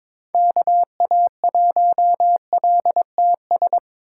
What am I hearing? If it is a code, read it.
Morse code, heard as a keyed tone, about 22 wpm, 700 Hz. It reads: KA1LTS
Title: USA Call Signs - 6 Characters - 4x Character Spacing - 22wpm